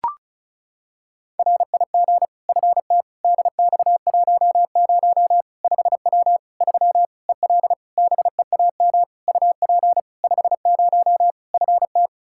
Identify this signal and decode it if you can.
Morse code, heard as a keyed tone, about 36 wpm, 700 Hz. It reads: RIG FT DX10 5W 3 EL BEAM UP 50 FT
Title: Practice QSO - Exchange 2 - Info - 35wpm